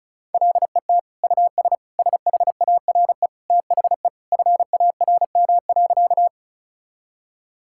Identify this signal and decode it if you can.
Morse code, heard as a keyed tone, about 36 wpm, 700 Hz. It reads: LET US SHARE THE FARM.
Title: Sentences from Top 1000 Words - Review Part 6 - 35wpm